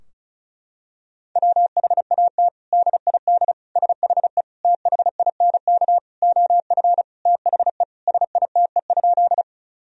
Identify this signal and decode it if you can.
Morse code, heard as a keyed tone, about 36 wpm, 700 Hz. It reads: WHAT DID SHE THINK OF THE SITE?
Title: Sentences from Top 1000 Words - Review Part 4 - 35wpm